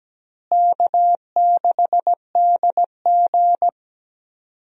Morse code, heard as a keyed tone, about 17 wpm, 700 Hz. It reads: K6DG
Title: USA Call Signs - 4 Characters - Repeated 3x - 17wpm